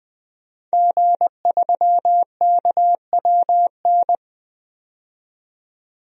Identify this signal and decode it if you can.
Morse code, heard as a keyed tone, about 21 wpm, 700 Hz. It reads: G3KWN
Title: DXCC-100 Call Signs - 5 Characters - 20wpm